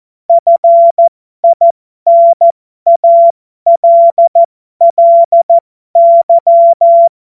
Morse code, heard as a keyed tone, about 14 wpm, spaced as for 12 wpm, 680 Hz. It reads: FINALLY
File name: Morsekurs / Witze